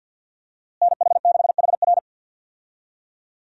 Morse code, heard as a keyed tone, about 47 wpm, 700 Hz. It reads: NH6HR
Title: USA Call Signs - 5 Characters - Repeated 3x - 50wpm